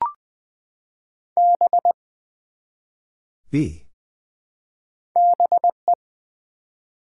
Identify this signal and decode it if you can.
Morse code, heard as a keyed tone, about 20 wpm, 700 Hz. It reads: B BE
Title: Build A Word - 1 Letter At A Time - 3 to 5-Letter Words - 20wpm